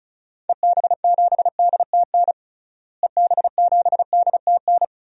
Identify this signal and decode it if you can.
Morse code, heard as a keyed tone, about 36 wpm, 700 Hz. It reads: EB7DTN EB7DTN
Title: DXCC-100 Call Signs - 6 Characters - Repeated 2x - 35wpm